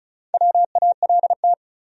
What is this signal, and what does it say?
Morse code, heard as a keyed tone, about 36 wpm, 700 Hz. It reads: WALT
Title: USA Common Names - Max 5 Letters - 35wpm